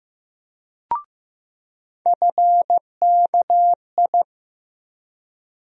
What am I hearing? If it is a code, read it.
Morse code, heard as a keyed tone, about 15 wpm, 700 Hz. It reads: FKI
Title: USA Call Sign Postfixes - 3 Letters 15wpm